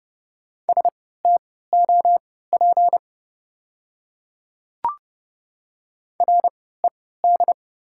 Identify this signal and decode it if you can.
Morse code, heard as a keyed tone, about 30 wpm, spaced as for 17 wpm, 690 Hz. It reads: STOP RED
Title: 3x Character Spacing - Top 500 Words - 30wpm